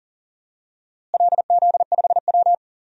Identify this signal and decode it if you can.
Morse code, heard as a keyed tone, about 40 wpm, 700 Hz. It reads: LZ5W
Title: DXCC-100 Call Signs - 4 Characters - Repeated 3x - 40wpm